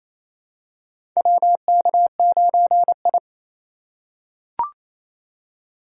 Morse code, heard as a keyed tone, about 28 wpm, 700 Hz. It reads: WK9I
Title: USA Call Signs - 4 Characters - 4x Character Spacing - 28wpm